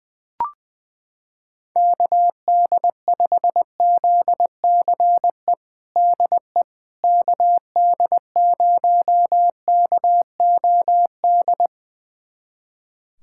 Morse code, heard as a keyed tone, about 20 wpm, 700 Hz. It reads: KD5ZCE DE KD0KOD
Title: USA Callsign DE Callsign - 6 Characters - 20wpm